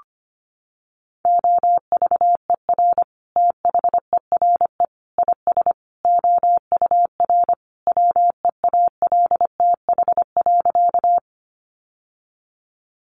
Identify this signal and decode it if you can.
Morse code, heard as a keyed tone, about 25 wpm, 700 Hz. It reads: OVER THERE IS OUR WEALTH.
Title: Sentences from Top 1200 Words 25wpm